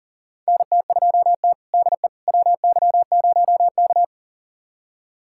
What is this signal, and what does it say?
Morse code, heard as a keyed tone, about 40 wpm, 700 Hz. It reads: NT2T DE WY0K
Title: USA Callsign DE Callsign - 4 Characters - 40wpm